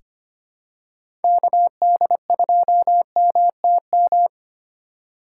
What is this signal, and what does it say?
Morse code, heard as a keyed tone, about 25 wpm, 700 Hz. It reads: KD2MTM
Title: USA Call Signs - 6 Characters - 6x Character Spacing - 25wpm